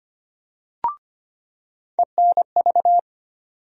Morse code, heard as a keyed tone, about 25 wpm, 700 Hz. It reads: ENV
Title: USA Call Sign Postfixes - 3 Letters 25wpm